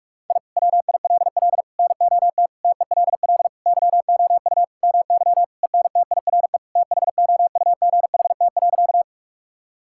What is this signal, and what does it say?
Morse code, heard as a keyed tone, about 46 wpm, 700 Hz. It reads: I WILL NOT TELL YOU MY ENTIRE THOUGHT.
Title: Sentences from Top 1000 Words - Review Part 5 - 45wpm